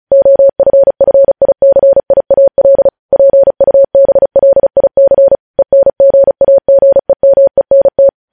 Morse code, heard as a keyed tone, about 36 wpm, 550 Hz. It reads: OFFICIAL PUBLIC ENGAGEMENT